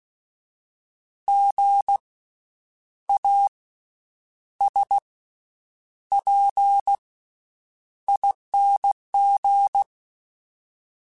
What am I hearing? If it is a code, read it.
Morse code, heard as a keyed tone, about 16 wpm, 800 Hz. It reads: G A S P ING
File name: Special INGs-mp3s / Special INGs-mp3s / Slow ing @15